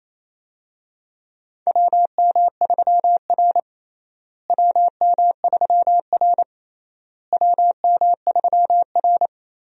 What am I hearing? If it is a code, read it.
Morse code, heard as a keyed tone, about 27 wpm, 700 Hz. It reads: WM3R WM3R WM3R
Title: USA Call Signs - 4 Characters - Repeated 3x - 28wpm